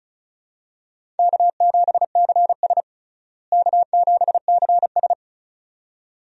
Morse code, heard as a keyed tone, about 36 wpm, 700 Hz. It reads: K7CS K7CS
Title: USA Call Signs - 4 Characters - Repeated 2x - 35wpm